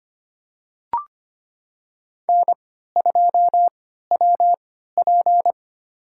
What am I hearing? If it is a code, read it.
Morse code, heard as a keyed tone, about 25 wpm, spaced as for 14 wpm, 700 Hz. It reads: N2WP
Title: USA Call Signs - 4 Characters - 3x Character Spacing - 25wpm